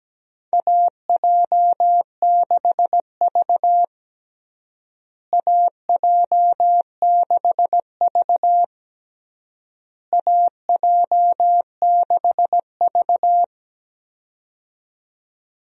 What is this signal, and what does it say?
Morse code, heard as a keyed tone, about 17 wpm, 700 Hz. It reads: AJ6V AJ6V AJ6V
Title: USA Call Signs - 4 Characters - Repeated 3x - 17wpm